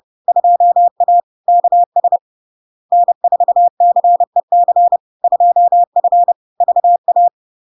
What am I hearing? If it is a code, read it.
Morse code, heard as a keyed tone, about 30 wpm, 700 Hz. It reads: 2A KS N4CEC 2F VA
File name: LARK-FD-2022-Log-in-CW-30wpm-700-Hz